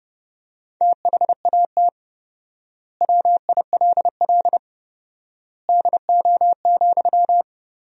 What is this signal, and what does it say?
Morse code, heard as a keyed tone, about 30 wpm, 700 Hz. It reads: THAT WILL DO,
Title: 4x Word-Spacing - Sentences from Top 100 Words - 30wpm